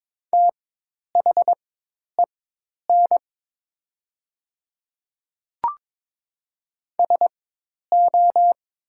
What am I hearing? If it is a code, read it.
Morse code, heard as a keyed tone, about 22 wpm, spaced as for 10 wpm, 700 Hz. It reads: THEN SO
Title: 4x Character Spacing - Top 100 Words - 22wpm